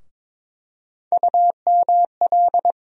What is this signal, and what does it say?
Morse code, heard as a keyed tone, about 22 wpm, 700 Hz. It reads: UML
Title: USA Call Sign Postfixes - 3 Letters 22wpm